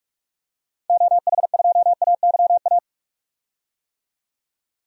Morse code, heard as a keyed tone, about 41 wpm, 700 Hz. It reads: OH2AYA
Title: DXCC-100 Call Signs - 6 Characters - 45wpm